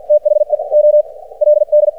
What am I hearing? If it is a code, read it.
Morse code, heard as a keyed tone, about 53 wpm, spaced as for 34 wpm, 600 Hz. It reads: THE OLD
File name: archived / CRSnet